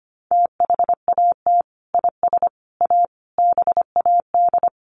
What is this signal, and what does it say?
Morse code, heard as a keyed tone, about 25 wpm, 700 Hz. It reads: THAT IS A BAD
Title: Sentences from Top 700 Words 25wpm